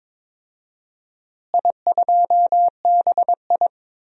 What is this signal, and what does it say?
Morse code, heard as a keyed tone, about 22 wpm, 700 Hz. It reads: I2BI
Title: DXCC-100 Call Signs - 4 Characters - Repeated 2x - 22wpm